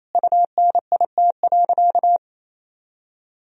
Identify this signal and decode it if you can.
Morse code, heard as a keyed tone, about 28 wpm, 700 Hz. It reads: UNIT.
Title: Sentences from Top 1000 Words - Review Part 2 - 28wpm